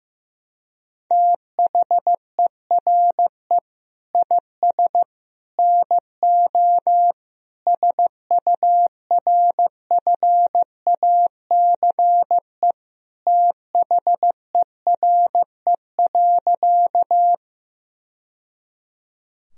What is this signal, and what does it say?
Morse code, heard as a keyed tone, about 16 wpm, 700 Hz. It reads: THERE IS NO SURFACE THERE.
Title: Sentences from Top 1000 Words - Review Part 3 - 15wpm